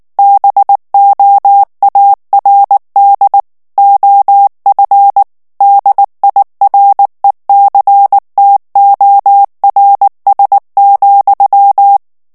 Morse code, heard as a keyed tone, about 19 wpm, 800 Hz. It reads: BOARD OF DIRECTORS,
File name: ANGLAIS texte clair 3 vitesse 18 mots minute